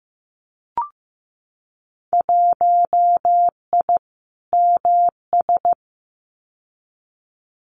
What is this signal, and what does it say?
Morse code, heard as a keyed tone, about 15 wpm, 700 Hz. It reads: 1I MS
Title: Winter Field Day Exchanges Encore 15wpm